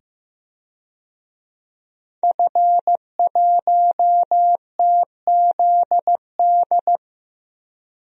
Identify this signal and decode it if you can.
Morse code, heard as a keyed tone, about 15 wpm, 700 Hz. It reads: F1TZD
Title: DXCC-100 Call Signs - 5 Characters - 15wpm